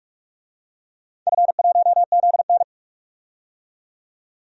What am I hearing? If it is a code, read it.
Morse code, heard as a keyed tone, about 46 wpm, 700 Hz. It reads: F1ZN
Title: DXCC-100 Call Signs - 4 Characters - 45wpm